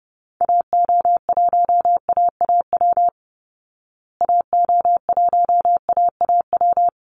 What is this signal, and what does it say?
Morse code, heard as a keyed tone, about 30 wpm, 700 Hz. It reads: AO1AAW AO1AAW
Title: Call Signs Repeated 3x - 30wpm